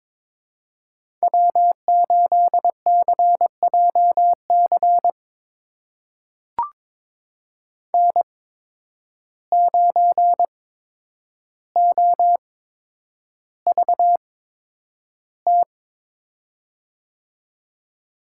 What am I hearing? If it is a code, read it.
Morse code, heard as a keyed tone, about 22 wpm, spaced as for 9 wpm, 700 Hz. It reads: W8CJC N9OVT
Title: USA Call Signs - 5 Characters - 8x Character Spacing - 22wpm